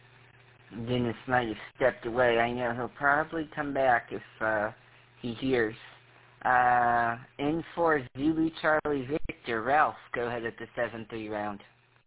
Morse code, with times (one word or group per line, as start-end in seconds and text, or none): none